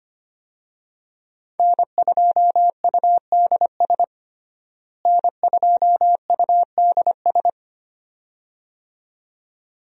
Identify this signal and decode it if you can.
Morse code, heard as a keyed tone, about 25 wpm, 700 Hz. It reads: N2UDS N2UDS
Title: USA Call Signs - 5 Characters - Repeated 2x - 25wpm